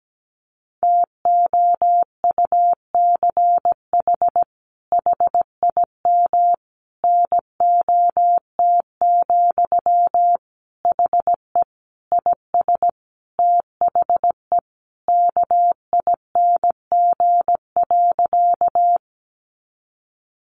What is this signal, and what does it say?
Morse code, heard as a keyed tone, about 17 wpm, 700 Hz. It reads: TOUCH HIM NOT, HE IS THE KING.
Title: Sentences from Top 1200 Words 17wpm